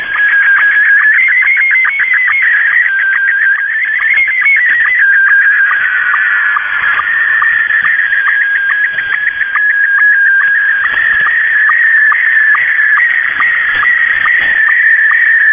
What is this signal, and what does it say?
SSTV (slow-scan television), Scottie 1.
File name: mw / 0256 6950u wolverine radio inc sstv